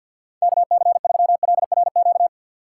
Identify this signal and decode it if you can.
Morse code, heard as a keyed tone, about 64 wpm, 700 Hz. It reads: KC3LRQ